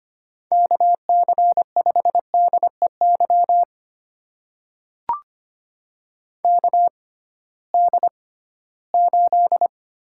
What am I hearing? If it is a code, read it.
Morse code, heard as a keyed tone, about 25 wpm, spaced as for 16 wpm, 700 Hz. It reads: KC5DEY KD8
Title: USA Call Signs - 6 Characters - 6x Character Spacing - 25wpm